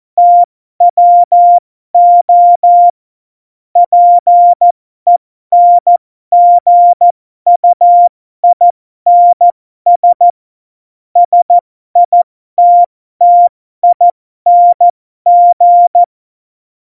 Morse code, heard as a keyed tone, about 14 wpm, spaced as for 12 wpm, 700 Hz. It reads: TWO PENGUINS SITTING